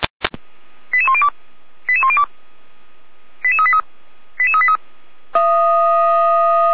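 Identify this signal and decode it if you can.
Five-tone selective call (ZVEI), heard as quick runs of five tones, tones of about 70 ms. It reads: 88102 88102 88383 88383
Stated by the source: ZVEI five-tone selective call 89165 received by SDR.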